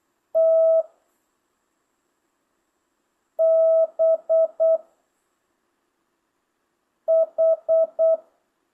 Morse code, heard as a keyed tone, about 8 wpm, 650 Hz. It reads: T B H
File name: jh8cbh / 2025cw